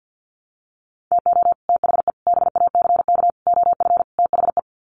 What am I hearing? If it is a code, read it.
Morse code, heard as a keyed tone, about 48 wpm, 700 Hz. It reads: TO THE BACK OF THE
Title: Sets of 5 Words 50wpm